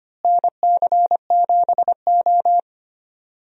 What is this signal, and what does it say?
Morse code, heard as a keyed tone, about 25 wpm, 700 Hz. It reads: NC7O
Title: USA Call Signs - 4 Characters - Repeated 3x - 25wpm